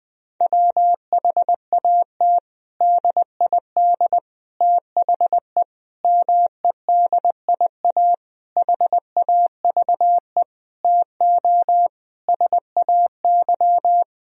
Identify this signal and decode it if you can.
Morse code, heard as a keyed tone, about 21 wpm, 700 Hz. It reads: WHAT DID THE MEDIA HAVE TO SAY
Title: Sentences from Top 1000 Words - Review Part 4 - 20wpm